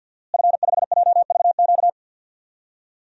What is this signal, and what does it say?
Morse code, heard as a keyed tone, about 53 wpm, 700 Hz. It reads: F5JVQ
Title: DXCC-100 Call Signs - 5 Characters - Repeated 2x - 50wpm